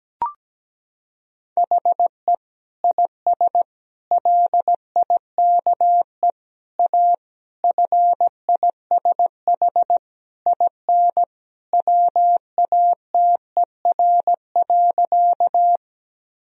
Morse code, heard as a keyed tone, about 17 wpm, 700 Hz. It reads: HE IS LIKE A FISH IN WATER.
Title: Sentences from Top 1000 Words - Review Part 1 - 17wpm